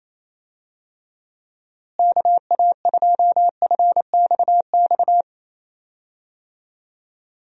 Morse code, heard as a keyed tone, about 28 wpm, 700 Hz. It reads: KA2FXX
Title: USA Call Signs - 6 Characters - Repeated 3x - 28wpm